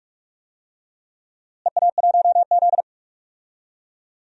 Morse code, heard as a keyed tone, about 42 wpm, 700 Hz. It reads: EA1Z
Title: DXCC-100 Call Signs - 4 Characters - Repeated 3x - 45wpm